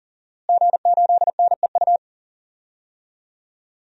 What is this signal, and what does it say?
Morse code, heard as a keyed tone, about 37 wpm, 700 Hz. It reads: G8NEU
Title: DXCC-100 Call Signs - 5 Characters - Repeated 2x - 40wpm